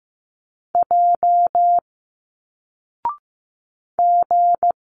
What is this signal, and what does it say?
Morse code, heard as a keyed tone, about 15 wpm, 700 Hz. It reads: J G
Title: Single Letters 15wpm